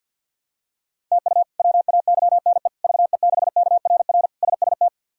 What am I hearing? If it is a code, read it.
Morse code, heard as a keyed tone, about 47 wpm, 700 Hz. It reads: TU WAYNE VE6KRR SST
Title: K1USN SST QSO Practice - 50wpm